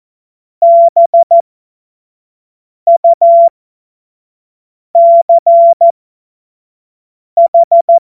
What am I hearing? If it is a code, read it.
Morse code, heard as a keyed tone, about 14 wpm, spaced as for 5 wpm, 680 Hz. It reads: BUCH
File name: Morsekurs / Witze / Witz 26. Oktober 2021, 25 F70